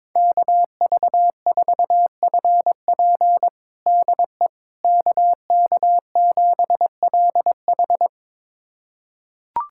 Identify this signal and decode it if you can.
Morse code, heard as a keyed tone, about 22 wpm, 700 Hz. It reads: KV4FP DE KK7LH
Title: USA Callsign DE Callsign - 5 Characters - 22wpm